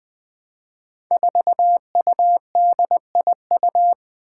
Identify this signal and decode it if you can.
Morse code, heard as a keyed tone, about 20 wpm, 700 Hz. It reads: 4UDIU
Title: Five Challenging Letter-Number Combinations 20wpm